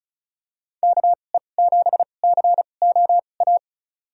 Morse code, heard as a keyed tone, about 36 wpm, spaced as for 25 wpm, 700 Hz. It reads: KE7COA
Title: USA Call Signs - 6 Characters - 2x Character Spacing - 35wpm